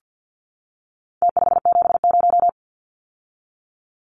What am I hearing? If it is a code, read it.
Morse code, heard as a keyed tone, about 50 wpm, 700 Hz. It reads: T570
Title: Numbers - Contest Style - 4-Digit 50wpm